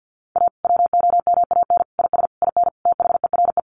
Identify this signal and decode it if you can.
Morse code, heard as a keyed tone, about 50 wpm, 700 Hz. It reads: A WOMAN IS IN THERE
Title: Sentences from Top 1400 Words 50wpm